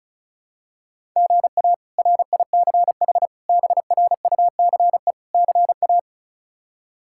Morse code, heard as a keyed tone, about 34 wpm, 700 Hz. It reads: GA RICH BRUCE CA
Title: K1USN SST QSO Practice - 35wpm